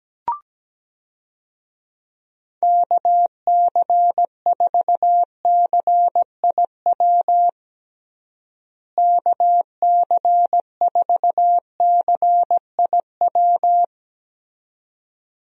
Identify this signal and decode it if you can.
Morse code, heard as a keyed tone, about 17 wpm, 700 Hz. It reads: KC4CIW KC4CIW
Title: USA Call Signs - 6 Characters - Repeated 2x - 17wpm